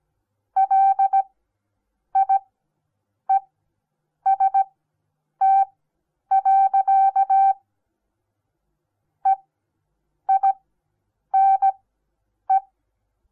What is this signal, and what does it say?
Morse code, heard as a keyed tone, about 17 wpm, spaced as for 8 wpm, 780 Hz. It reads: LIEST. EINE
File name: Onlinelehrgang / morsen / DJ4UF Morsen08